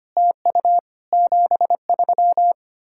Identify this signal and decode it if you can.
Morse code, heard as a keyed tone, about 33 wpm, 700 Hz. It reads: TU 73